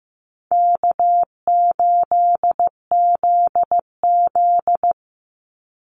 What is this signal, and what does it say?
Morse code, heard as a keyed tone, about 15 wpm, 700 Hz. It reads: K8ZZ
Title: Easy Call Signs Repeated 2x - 15wpm